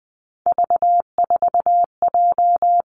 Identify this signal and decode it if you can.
Morse code, heard as a keyed tone, about 20 wpm, 700 Hz. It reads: V4J
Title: Three Challenging Letter-Number Combinations 20wpm